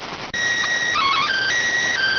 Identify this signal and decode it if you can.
SSTV (slow-scan television), PD 290.